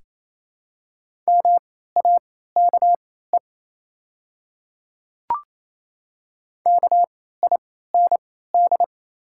Morse code, heard as a keyed tone, about 28 wpm, spaced as for 16 wpm, 700 Hz. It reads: MAKE KIND
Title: 3x Character Spacing - Top 300 Words - 28wpm